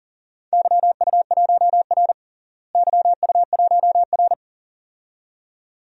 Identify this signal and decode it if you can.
Morse code, heard as a keyed tone, about 37 wpm, 700 Hz. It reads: YU1R YU1R
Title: DXCC-100 Call Signs - 4 Characters - Repeated 2x - 40wpm